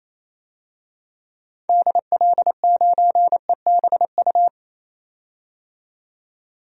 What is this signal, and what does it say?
Morse code, heard as a keyed tone, about 28 wpm, 700 Hz. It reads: DL9EBU
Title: DXCC-100 Call Signs - 6 Characters - 28wpm